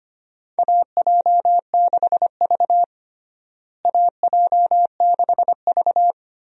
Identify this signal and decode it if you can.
Morse code, heard as a keyed tone, about 25 wpm, 700 Hz. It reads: AJ6V AJ6V
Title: USA Call Signs - 4 Characters - Repeated 3x - 25wpm